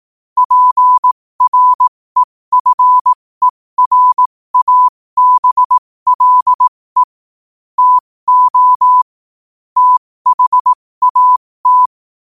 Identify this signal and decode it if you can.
Morse code, heard as a keyed tone, about 18 wpm, spaced as for 15 wpm, 1000 Hz. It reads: PREFERABLE TO THAT
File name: 15wpm